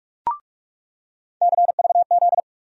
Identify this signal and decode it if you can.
Morse code, heard as a keyed tone, about 46 wpm, 700 Hz. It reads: CVZ
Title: USA Call Sign Postfixes - 3 Letters 45wpm